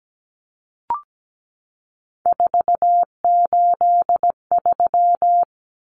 Morse code, heard as a keyed tone, about 17 wpm, 700 Hz. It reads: 483
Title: Numbers - 3-Digit 17wpm